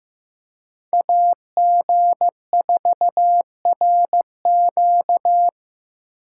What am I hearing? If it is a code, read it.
Morse code, heard as a keyed tone, about 18 wpm, 700 Hz. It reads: AG4RQ
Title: Speed Racing - USA 2x2 Call Signs 15wpm